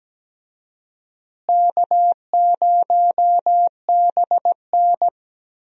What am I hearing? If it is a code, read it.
Morse code, heard as a keyed tone, about 17 wpm, 700 Hz. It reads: K0BN
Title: USA Call Signs - 4 Characters - Repeated 3x - 17wpm